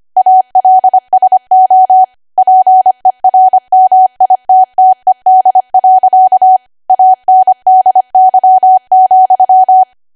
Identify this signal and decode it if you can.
Morse code, heard as a keyed tone, about 25 wpm, 750 Hz. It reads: ALSO PERMITTED. ANDY,
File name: Archive / 25 WPM / 25WPM